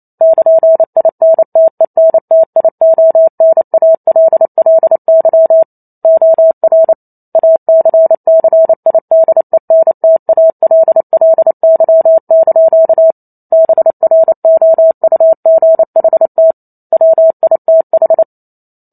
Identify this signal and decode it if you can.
Morse code, heard as a keyed tone, about 28 wpm, 650 Hz. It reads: <KN>INTENTIONALLY OR ACCIDENTALLY) BROUGHT WITH
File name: Chapter 2 - 28 WPM